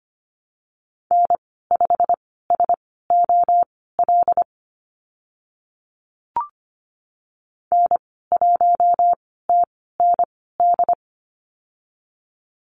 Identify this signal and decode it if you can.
Morse code, heard as a keyed tone, about 25 wpm, spaced as for 16 wpm, 690 Hz. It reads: N5SOL N1TND
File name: Lesson 15g - Number 5 - Call Signs 25wpm